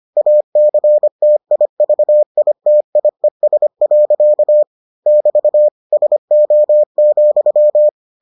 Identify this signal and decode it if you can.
Morse code, heard as a keyed tone, about 25 wpm, 600 Hz. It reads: ACTIVITIES. <BT> SO,